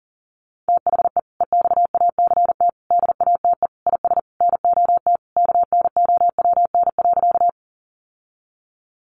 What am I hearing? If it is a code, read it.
Morse code, heard as a keyed tone, about 40 wpm, 700 Hz. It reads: THE EXACT DATE IS NOT KNOWN.
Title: Sentences from Top 1300 Words 40wpm